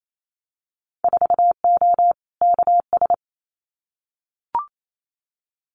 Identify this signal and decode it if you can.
Morse code, heard as a keyed tone, about 28 wpm, 700 Hz. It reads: 4O KS
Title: Winter Field Day Exchanges Encore 28wpm